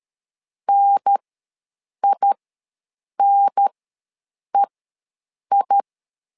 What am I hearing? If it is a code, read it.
Morse code, heard as a keyed tone, about 13 wpm, spaced as for 7 wpm, 800 Hz. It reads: NINEI
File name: LEZIONI / 2 UHDN / 2 UHDN ws 12wpm 1708